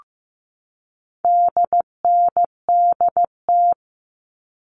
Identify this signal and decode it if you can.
Morse code, heard as a keyed tone, about 15 wpm, 700 Hz. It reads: DNDT
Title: All Cut Numbers - 4-Digit 15wpm